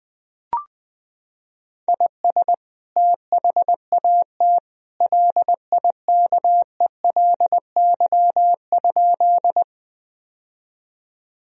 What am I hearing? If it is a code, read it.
Morse code, heard as a keyed tone, about 20 wpm, 700 Hz. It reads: IS THAT LIKELY?
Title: Sentences from Top 1000 Words - Review Part 4 - 20wpm